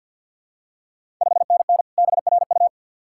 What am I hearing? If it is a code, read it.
Morse code, heard as a keyed tone, about 50 wpm, 700 Hz. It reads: 5NN BRU
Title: Ontario QSO Party Exchange 50wpm